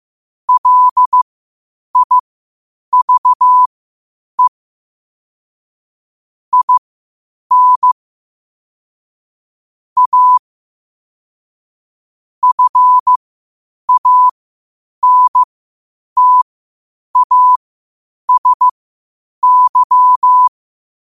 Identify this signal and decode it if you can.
Morse code, heard as a keyed tone, about 15 wpm, spaced as for 8 wpm, 1000 Hz. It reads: LIVE IN A FANTASY
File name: qotdfiles / 8wpm